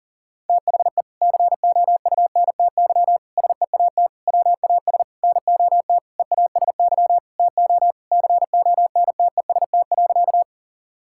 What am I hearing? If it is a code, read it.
Morse code, heard as a keyed tone, about 40 wpm, 700 Hz. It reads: THE COUNTY SEAT WAS NOT EASY TO CONTEST.
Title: Sentences from Top 1600 Words 40wpm